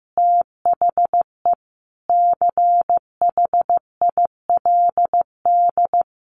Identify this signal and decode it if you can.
Morse code, heard as a keyed tone, about 15 wpm, 700 Hz. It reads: THE CHILD
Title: Additional Easy Phrases 15wpm